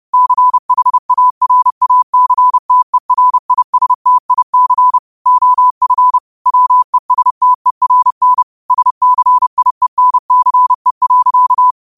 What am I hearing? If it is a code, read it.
Morse code, heard as a keyed tone, about 30 wpm, 1000 Hz. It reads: CHARACTERISTIC OF WESTERN SCIENCE.